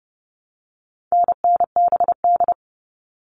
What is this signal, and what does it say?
Morse code, heard as a keyed tone, about 30 wpm, 700 Hz. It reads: NNBD
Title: All Cut Numbers - 4-Digit 30wpm